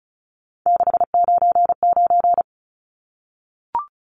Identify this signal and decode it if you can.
Morse code, heard as a keyed tone, about 34 wpm, 700 Hz. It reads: 699
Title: Numbers - 3-Digit 35wpm